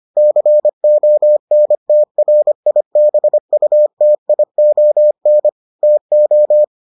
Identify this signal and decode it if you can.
Morse code, heard as a keyed tone, about 25 wpm, 600 Hz. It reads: CONTRIBUTION TO